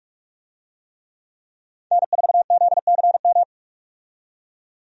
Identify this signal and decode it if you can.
Morse code, heard as a keyed tone, about 46 wpm, 700 Hz. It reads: N4ZCM
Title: USA Call Signs - 5 Characters - 45wpm